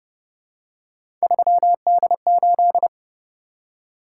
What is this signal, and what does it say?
Morse code, heard as a keyed tone, about 30 wpm, 700 Hz. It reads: 3D8